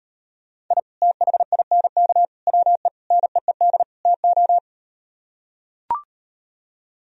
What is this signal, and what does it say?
Morse code, heard as a keyed tone, about 39 wpm, 700 Hz. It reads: I THINK WE NEED TO